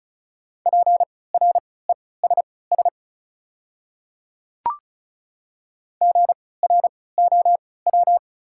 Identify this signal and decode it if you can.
Morse code, heard as a keyed tone, about 36 wpm, spaced as for 20 wpm, 700 Hz. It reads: PRESS GROW
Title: 3x Character Spacing - Top 300 Words - 35wpm